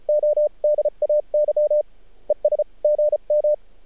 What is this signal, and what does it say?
Morse code, heard as a keyed tone, about 34 wpm, 600 Hz. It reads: O DAY ES GM
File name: archived / CRSnet